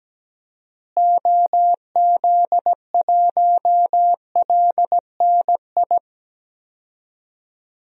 Morse code, heard as a keyed tone, about 17 wpm, 700 Hz. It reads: OZ1LNI
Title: DXCC-100 Call Signs - 6 Characters - Repeated 3x - 17wpm